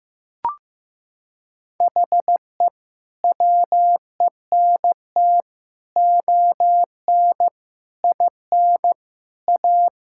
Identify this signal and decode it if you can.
Morse code, heard as a keyed tone, about 16 wpm, 700 Hz. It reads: HE WENT ON IN A
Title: Sentences from Top 1600 Words 15wpm